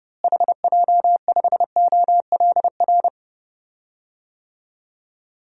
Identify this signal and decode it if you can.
Morse code, heard as a keyed tone, about 29 wpm, 700 Hz. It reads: HJ5OLR
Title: DXCC-100 Call Signs - 6 Characters - Repeated 3x - 30wpm